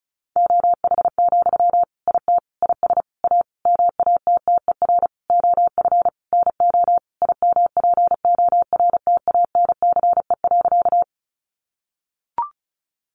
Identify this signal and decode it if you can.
Morse code, heard as a keyed tone, about 34 wpm, 700 Hz. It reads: OH, IT IS A MATTER OF NO IMPORTANCE.
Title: Sentences from Top 1200 Words 35wpm